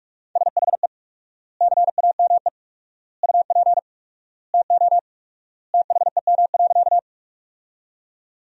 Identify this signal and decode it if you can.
Morse code, heard as a keyed tone, about 42 wpm, 700 Hz. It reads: SHE CAME UP TO THEM.
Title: 4x Word-Spacing - Sentences from Top 200 Words - 45wpm